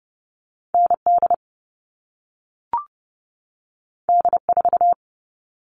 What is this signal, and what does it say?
Morse code, heard as a keyed tone, about 30 wpm, 700 Hz. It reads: ND D4
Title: All Cut Numbers - 2-Digit 30wpm